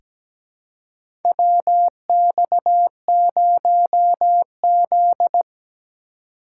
Morse code, heard as a keyed tone, about 17 wpm, 700 Hz. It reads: WX0Z
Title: K1USN SST QSO Practice - 17wpm